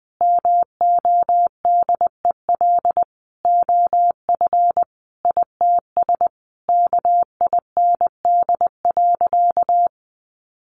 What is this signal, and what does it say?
Morse code, heard as a keyed tone, about 19 wpm, 700 Hz. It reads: MODEL OF ITS KIND.
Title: Sentences from Top 1000 Words 20wpm